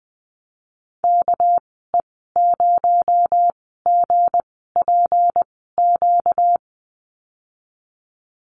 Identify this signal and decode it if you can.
Morse code, heard as a keyed tone, about 20 wpm, spaced as for 14 wpm, 700 Hz. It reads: KE0GPQ
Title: 245 Lesson 39g - Letter z - Call Signs 20wpm